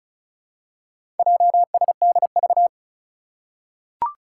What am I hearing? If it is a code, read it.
Morse code, heard as a keyed tone, about 36 wpm, 700 Hz. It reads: JSDV
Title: Four Challenging Letter-Number Combinations 35wpm